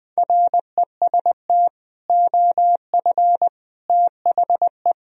Morse code, heard as a keyed tone, about 20 wpm, 700 Hz. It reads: REST OF THE
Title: Speed-Racing - Sets of 4 Words 17wpm